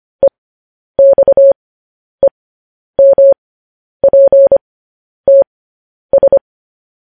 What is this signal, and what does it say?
Morse code, heard as a keyed tone, about 25 wpm, spaced as for 10 wpm, 550 Hz. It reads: EXEMPTS
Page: news headlines in morse code at 10 wpm